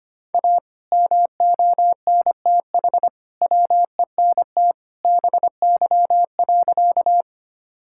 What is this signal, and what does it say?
Morse code, heard as a keyed tone, about 25 wpm, 700 Hz. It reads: A MONTH WENT BY.
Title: Sentences from Top 1000 Words - Review Part 5 - 25wpm